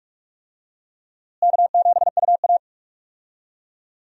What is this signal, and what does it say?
Morse code, heard as a keyed tone, about 46 wpm, 700 Hz. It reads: K7UA
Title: USA Call Signs - 4 Characters - Repeated 3x - 45wpm